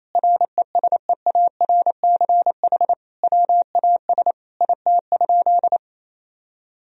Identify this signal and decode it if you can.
Morse code, heard as a keyed tone, about 28 wpm, 700 Hz. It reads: RESEARCH WAS IT?
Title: Sentences from Top 1000 Words - Review Part 4 - 28wpm